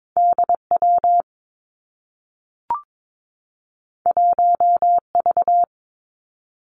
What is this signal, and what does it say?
Morse code, heard as a keyed tone, about 22 wpm, 700 Hz. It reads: D W 1 V
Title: Two Challenging Letter-Number Combinations 22wpm